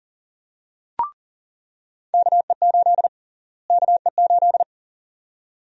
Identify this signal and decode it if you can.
Morse code, heard as a keyed tone, about 40 wpm, 700 Hz. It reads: KE8 KE8
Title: USA Call Sign Prefixes - Repeated 2x - 40wpm